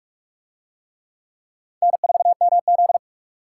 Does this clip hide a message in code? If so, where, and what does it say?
Yes: Morse code from 1.8 to 3.0 s, heard as a keyed tone, about 44 wpm, 700 Hz. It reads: N4MZ